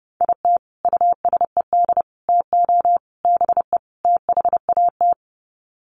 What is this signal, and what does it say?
Morse code, heard as a keyed tone, about 28 wpm, 700 Hz. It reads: IT USED TO BE THAT
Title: Sets of 5 Words Encore2 30wpm